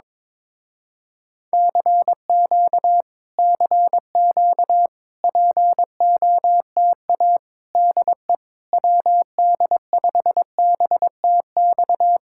Morse code, heard as a keyed tone, about 22 wpm, 700 Hz. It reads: CQ CQ POTA DE WD5BTX
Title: Speed-Racing Common QSO Phrases - 22wpm